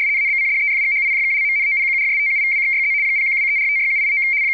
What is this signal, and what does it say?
RTTY, about 75 baud. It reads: ABCDEFGHIJKLMNOPQRSTUVWXYZ 0123456789
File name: rtty170 / rtty75.wav / rtty75